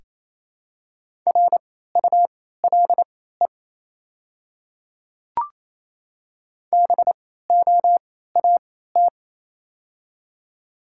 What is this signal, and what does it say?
Morse code, heard as a keyed tone, about 28 wpm, spaced as for 16 wpm, 700 Hz. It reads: RULE BOAT
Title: 3x Character Spacing - Top 500 Words - 28wpm